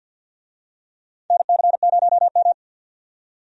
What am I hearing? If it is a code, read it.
Morse code, heard as a keyed tone, about 51 wpm, 700 Hz. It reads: NC0M
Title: USA Call Signs - 4 Characters - 50wpm